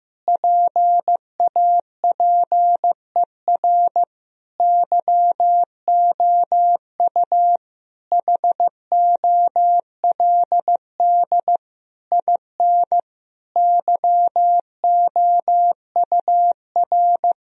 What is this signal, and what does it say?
Morse code, heard as a keyed tone, about 15 wpm, 700 Hz. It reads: PAPER YOU HOLD IN YOUR
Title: Sentences from Top 1000 Words - Review Part 2 - 15wpm